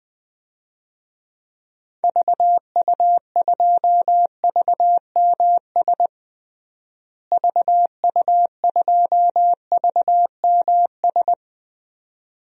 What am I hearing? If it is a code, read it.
Morse code, heard as a keyed tone, about 20 wpm, 700 Hz. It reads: VU2VMS VU2VMS
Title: DXCC-100 Call Signs - 6 Characters - Repeated 2x - 20wpm